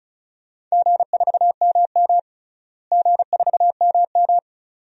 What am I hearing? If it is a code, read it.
Morse code, heard as a keyed tone, about 36 wpm, 700 Hz. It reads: G4MM G4MM
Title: DXCC-100 Call Signs - 4 Characters - Repeated 2x - 35wpm